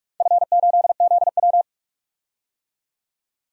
Morse code, heard as a keyed tone, about 43 wpm, 700 Hz. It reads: F8ZW
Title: DXCC-100 Call Signs - 4 Characters - 45wpm